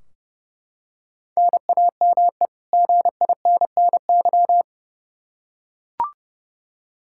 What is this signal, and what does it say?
Morse code, heard as a keyed tone, about 30 wpm, 700 Hz. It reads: NAME GINNY
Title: Practice QSO - Exchange 1 - Beginning - 30wpm